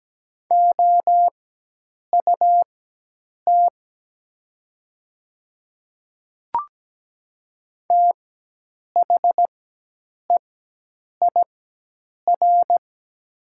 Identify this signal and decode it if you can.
Morse code, heard as a keyed tone, about 17 wpm, spaced as for 8 wpm, 700 Hz. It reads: OUT THEIR
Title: 4x Character Spacing - Top 100 Words - 17wpm